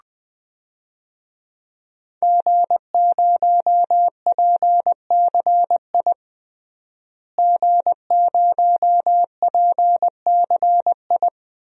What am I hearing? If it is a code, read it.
Morse code, heard as a keyed tone, about 20 wpm, 700 Hz. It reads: G0PCI G0PCI
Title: DXCC-100 Call Signs - 5 Characters - Repeated 3x - 20wpm